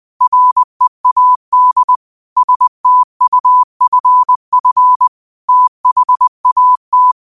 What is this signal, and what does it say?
Morse code, heard as a keyed tone, about 20 wpm, 1000 Hz. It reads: READ STUFF THAT
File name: qotdfiles / 20wpm